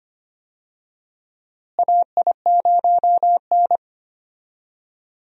Morse code, heard as a keyed tone, about 25 wpm, 700 Hz. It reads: AI0N
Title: USA Call Signs - 4 Characters - Repeated 2x - 25wpm